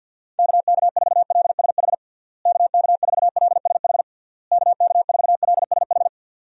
Morse code, heard as a keyed tone, about 49 wpm, 700 Hz. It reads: KK4LSH KK4LSH KK4LSH
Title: USA Call Signs - 6 Characters - Repeated 3x - 50wpm